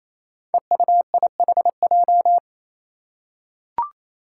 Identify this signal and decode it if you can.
Morse code, heard as a keyed tone, about 28 wpm, 700 Hz. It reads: EUIHJ
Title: Five Challenging Letter-Number Combinations 28wpm